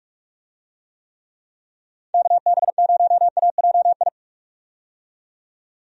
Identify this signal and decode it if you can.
Morse code, heard as a keyed tone, about 46 wpm, 700 Hz. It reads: KB0AJI
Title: USA Call Signs - 6 Characters - 45wpm